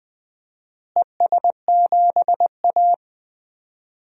Morse code, heard as a keyed tone, about 21 wpm, 700 Hz. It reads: ES7A
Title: DXCC-100 Call Signs - 4 Characters - Repeated 3x - 20wpm